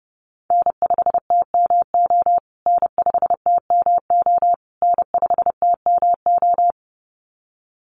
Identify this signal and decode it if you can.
Morse code, heard as a keyed tone, about 30 wpm, 700 Hz. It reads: N5TMO N5TMO N5TMO
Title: USA Call Signs Repeated 3x 30wpm